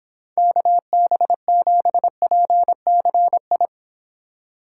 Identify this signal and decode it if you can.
Morse code, heard as a keyed tone, about 26 wpm, 700 Hz. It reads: KB7PCI